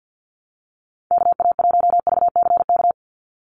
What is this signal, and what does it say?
Morse code, heard as a keyed tone, about 50 wpm, 700 Hz. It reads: KA1VCK
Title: USA Call Signs 50wpm